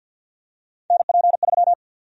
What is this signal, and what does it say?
Morse code, heard as a keyed tone, about 50 wpm, 700 Hz. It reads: NP3
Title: USA Call Sign Prefixes - 50wpm